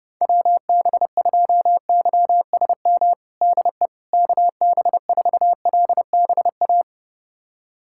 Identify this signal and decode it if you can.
Morse code, heard as a keyed tone, about 30 wpm, 700 Hz. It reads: WB2YSM DE KB4LBA
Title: USA Callsign DE Callsign 30wpm